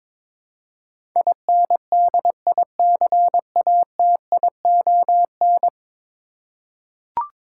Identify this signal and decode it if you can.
Morse code, heard as a keyed tone, about 22 wpm, 700 Hz. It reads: INDICATION
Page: morse-code-ninja-practice-22wpm